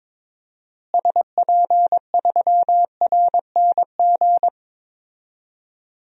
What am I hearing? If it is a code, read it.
Morse code, heard as a keyed tone, about 22 wpm, 700 Hz. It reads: SP3RNG
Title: DXCC-100 Call Signs - 6 Characters - Repeated 3x - 22wpm